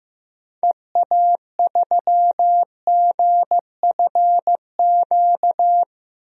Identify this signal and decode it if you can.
Morse code, heard as a keyed tone, about 15 wpm, 700 Hz. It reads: EA3GFQ
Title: DXCC-100 Call Signs - 6 Characters - 15wpm